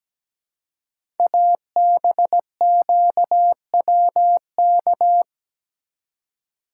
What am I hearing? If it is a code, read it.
Morse code, heard as a keyed tone, about 17 wpm, 700 Hz. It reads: ABQWK
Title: Code Groups - 5 Letters - 17wpm